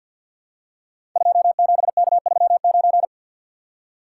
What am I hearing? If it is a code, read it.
Morse code, heard as a keyed tone, about 46 wpm, 700 Hz. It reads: 27K39
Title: Code Groups - 5 Letter:Numbers - 50wpm